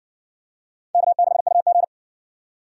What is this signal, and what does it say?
Morse code, heard as a keyed tone, about 57 wpm, 700 Hz. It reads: K6UG